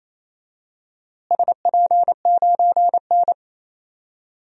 Morse code, heard as a keyed tone, about 28 wpm, 700 Hz. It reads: SP9N
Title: DXCC-100 Call Signs - 4 Characters - Repeated 3x - 28wpm